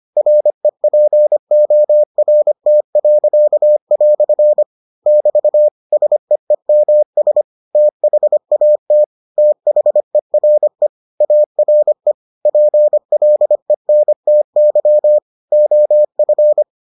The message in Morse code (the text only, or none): REPORT." <BT> SEEMS THAT THERE ARE PLENTY OF